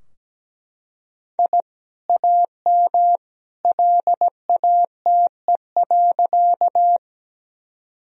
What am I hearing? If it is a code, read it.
Morse code, heard as a keyed tone, about 17 wpm, 700 Hz. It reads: I AM LATE.
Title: Sentences from Top 1000 Words - Review Part 2 - 17wpm